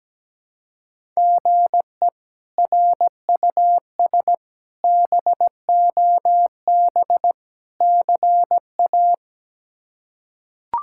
Morse code, heard as a keyed tone, about 17 wpm, 700 Hz. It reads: GE RUS BOB CA
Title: K1USN SST QSO Practice - 17wpm